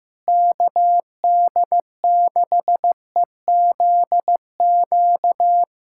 Morse code, heard as a keyed tone, about 15 wpm, 700 Hz. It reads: KD6EZQ
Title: USA Callsign DE Callsign - 6 Characters - 15wpm